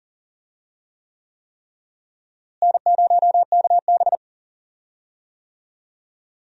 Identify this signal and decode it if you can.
Morse code, heard as a keyed tone, about 39 wpm, 700 Hz. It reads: N0KB
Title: USA Call Signs - 4 Characters - Repeated 3x - 40wpm